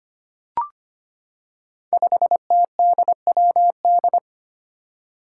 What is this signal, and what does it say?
Morse code, heard as a keyed tone, about 25 wpm, 700 Hz. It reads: 5TDWD
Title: Five Challenging Letter-Number Combinations 25wpm